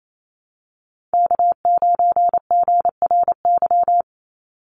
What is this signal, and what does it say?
Morse code, heard as a keyed tone, about 28 wpm, 700 Hz. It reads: K9GRY
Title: USA Call Signs 28wpm